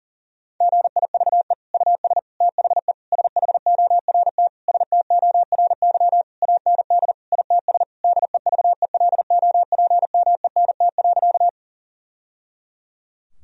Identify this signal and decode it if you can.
Morse code, heard as a keyed tone, about 40 wpm, 700 Hz. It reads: GIVE US THE SHORT STORY AND ITS DEVELOPMENT.
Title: Sentences from Top 1000 Words - Review Part 4 - 40wpm